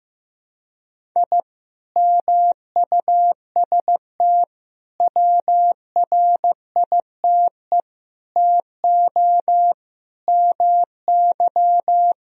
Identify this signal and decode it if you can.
Morse code, heard as a keyed tone, about 15 wpm, 700 Hz. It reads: I MUST WRITE TO MY
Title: Sentences from Top 1000 Words - Review Part 1 - 15wpm